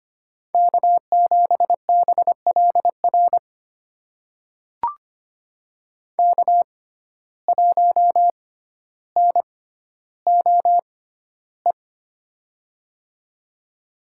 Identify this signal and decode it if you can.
Morse code, heard as a keyed tone, about 25 wpm, spaced as for 13 wpm, 700 Hz. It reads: K7BLR K1NOE
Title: USA Call Signs - 5 Characters - 6x Character Spacing - 25wpm